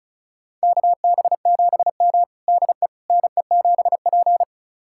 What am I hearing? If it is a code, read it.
Morse code, heard as a keyed tone, about 34 wpm, 700 Hz. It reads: KB7M DE NE7P
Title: USA Callsign DE Callsign - 4 Characters - 35wpm